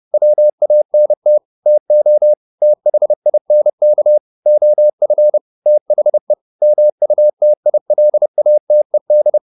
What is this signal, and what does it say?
Morse code, heard as a keyed tone, about 30 wpm, 600 Hz. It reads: WANT TO THINK OF THE MUTILATED